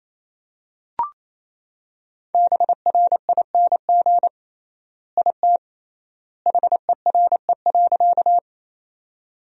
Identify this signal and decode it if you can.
Morse code, heard as a keyed tone, about 28 wpm, 700 Hz. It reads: BRING IT HERE.
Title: 3x Word-Spacing - Sentences from Top 500 Words - 28wpm